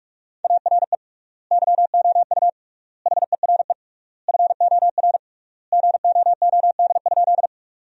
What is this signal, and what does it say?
Morse code, heard as a keyed tone, about 41 wpm, 700 Hz. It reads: ARE YOU HERE FOR GOOD?
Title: 3x Word-Spacing - Sentences from Top 200 Words - 45wpm